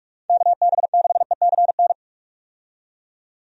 Morse code, heard as a keyed tone, about 41 wpm, 700 Hz. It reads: KB6ECN
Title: USA Call Signs - 6 Characters - 45wpm